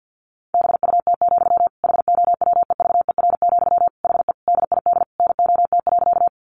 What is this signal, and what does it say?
Morse code, heard as a keyed tone, about 50 wpm, 700 Hz. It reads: BUT, HOWEVER, HE DID NOT.
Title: Sentences from Top 600 Words 50wpm